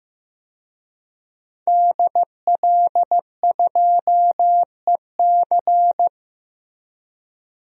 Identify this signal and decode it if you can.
Morse code, heard as a keyed tone, about 15 wpm, 700 Hz. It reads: DL2EC
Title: DXCC-100 Call Signs - 5 Characters - Repeated 3x - 15wpm